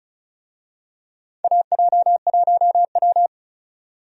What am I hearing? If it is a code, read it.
Morse code, heard as a keyed tone, about 34 wpm, 700 Hz. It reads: AJ1W
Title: USA Call Signs - 4 Characters - 35wpm